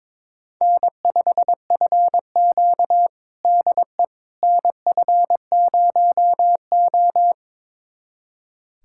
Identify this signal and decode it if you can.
Morse code, heard as a keyed tone, about 22 wpm, 700 Hz. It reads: N5FQ DE NF0O
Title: USA Callsign DE Callsign - 4 Characters - 22wpm